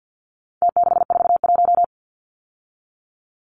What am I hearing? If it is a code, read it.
Morse code, heard as a keyed tone, about 50 wpm, 700 Hz. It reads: T641
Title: Numbers - Contest Style - 4-Digit 50wpm